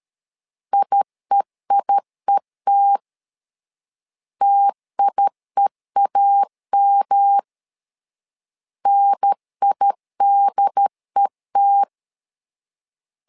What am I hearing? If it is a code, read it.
Morse code, heard as a keyed tone, about 13 wpm, 800 Hz. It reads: IEIET TIEAM NIDET
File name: LEZIONI / 3 MWRO / 3 M 12wpm 1541